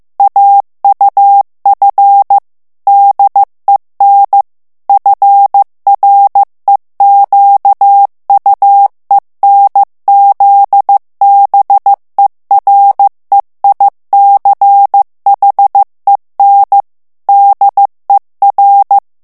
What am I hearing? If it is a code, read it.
Morse code, heard as a keyed tone, about 15 wpm, 800 Hz. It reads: AUF DEN FREQUENZBEREICHEN DER
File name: morse mp3 / ALLEMAND texte clair 1 vitesse 14 mots minute